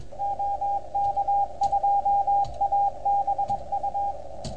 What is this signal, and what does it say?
Morse code, heard as a keyed tone, about 21 wpm, 800 Hz, with radio noise. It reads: OK2ABU
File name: Archiv / CUC310 / OK1FGD CUC 310 pileup